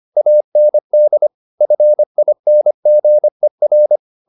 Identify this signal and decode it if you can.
Morse code, heard as a keyed tone, about 25 wpm, 600 Hz. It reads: AND FINGER
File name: the old man and the sea part 1 25wpm